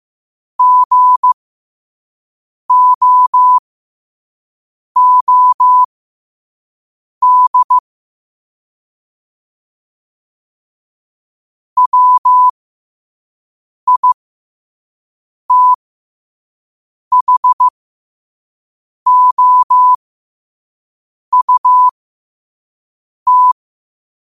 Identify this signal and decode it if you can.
Morse code, heard as a keyed tone, about 15 wpm, spaced as for 5 wpm, 1000 Hz. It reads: GOOD WITHOUT
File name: qotdfiles / 5wpm